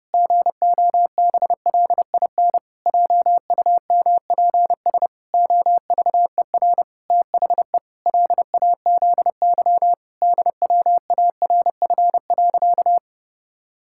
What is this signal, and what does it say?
Morse code, heard as a keyed tone, about 30 wpm, 700 Hz. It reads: GOBLIN JUMPS OVER THE LAZY DWARF.
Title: Pangram Challenge v2 - 30wpm